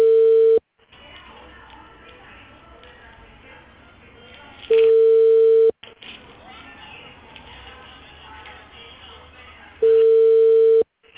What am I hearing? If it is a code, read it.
Ringback tone (China, 450 Hz), with speech.